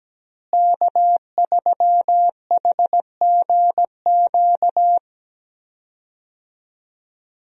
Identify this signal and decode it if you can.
Morse code, heard as a keyed tone, about 17 wpm, 700 Hz. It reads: K3HGQ
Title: USA Call Signs - 5 Characters - Repeated 3x - 17wpm